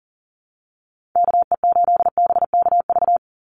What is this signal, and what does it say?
Morse code, heard as a keyed tone, about 41 wpm, 700 Hz. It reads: KE8BKV
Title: USA Call Signs 40wpm